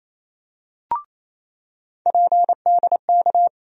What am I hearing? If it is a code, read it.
Morse code, heard as a keyed tone, about 28 wpm, 700 Hz. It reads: PDK
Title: USA Call Sign Postfixes - 3 Letters 28wpm